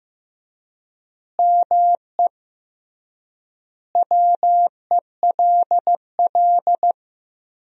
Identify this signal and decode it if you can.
Morse code, heard as a keyed tone, about 15 wpm, 700 Hz. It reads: ME WELL
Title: 3x Word-Spacing - Sentences from Top 400 Words - 15wpm